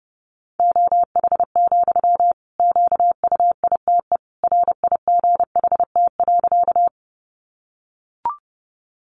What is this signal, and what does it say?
Morse code, heard as a keyed tone, about 30 wpm, 700 Hz. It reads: OH, QUITE RIGHT.
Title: Sentences from Top 700 Words 30wpm